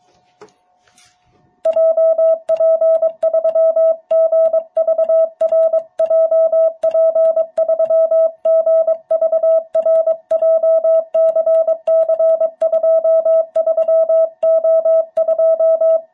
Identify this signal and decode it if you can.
Morse code, heard as a keyed tone, about 23 wpm, 660 Hz. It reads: JP3GVR JP3GVR JCC23O2
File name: jh8cbh / 2021cw